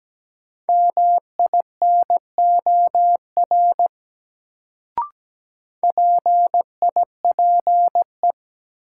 Morse code, heard as a keyed tone, about 17 wpm, 700 Hz. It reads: MINOR PIPE
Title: Sentences from Top 1500 Words - New Word Drill - 17wpm